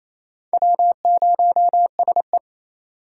Morse code, heard as a keyed tone, about 28 wpm, 700 Hz. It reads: W0SE
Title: USA Call Signs - 4 Characters - 4x Character Spacing - 28wpm